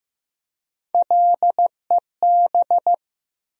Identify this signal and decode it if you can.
Morse code, heard as a keyed tone, about 15 wpm, 700 Hz. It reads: LEB
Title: USA Call Sign Postfixes - 3 Letters 15wpm